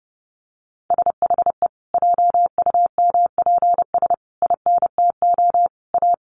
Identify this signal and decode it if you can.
Morse code, heard as a keyed tone, about 29 wpm, 700 Hz. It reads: SHE JUMPS INTO A
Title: Easy Phrases 30wpm